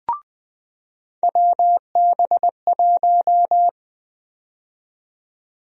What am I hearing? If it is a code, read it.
Morse code, heard as a keyed tone, about 20 wpm, 700 Hz. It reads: WB1
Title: USA Call Sign Prefixes - 20wpm